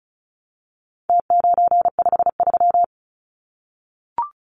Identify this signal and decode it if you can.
Morse code, heard as a keyed tone, about 34 wpm, 700 Hz. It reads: T953
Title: Numbers - Contest Style - 4-Digit 35wpm